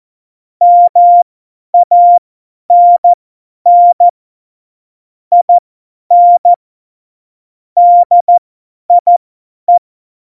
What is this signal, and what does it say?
Morse code, heard as a keyed tone, about 14 wpm, spaced as for 10 wpm, 700 Hz. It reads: MANN IN DIE
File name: Morsekurs / Witze / Witz 30. Oktober 2018, 50 F70